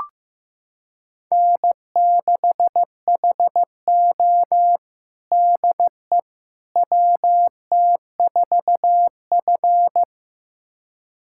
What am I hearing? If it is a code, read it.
Morse code, heard as a keyed tone, about 16 wpm, 700 Hz. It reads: N6HO DE WT4F
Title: USA Callsign DE Callsign - 4 Characters - 15wpm